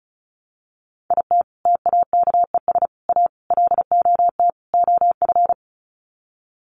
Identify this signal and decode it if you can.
Morse code, heard as a keyed tone, about 34 wpm, 700 Hz. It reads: IT TAKES A LOT OF
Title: Sets of 5 Words Encore 35wpm